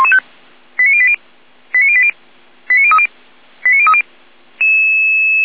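Five-tone selective call (ZVEI), heard as quick runs of five tones, tones of about 70 ms. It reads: 89080 89080 89030 89030
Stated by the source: ZVEI five-tone selective call 89117 received on SDR.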